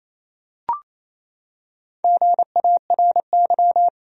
Morse code, heard as a keyed tone, about 28 wpm, 700 Hz. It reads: GARY
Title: USA Common Names - Max 4 Letters - 28wpm